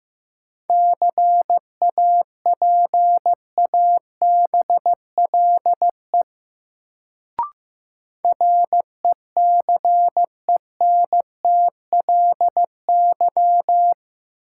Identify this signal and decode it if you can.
Morse code, heard as a keyed tone, about 15 wpm, 700 Hz. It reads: CAPABLE RECENTLY
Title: Sentences from Top 1200 Words - New Word Drill - 15wpm